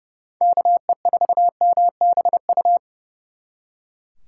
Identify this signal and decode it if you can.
Morse code, heard as a keyed tone, about 39 wpm, 700 Hz. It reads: KE4MBU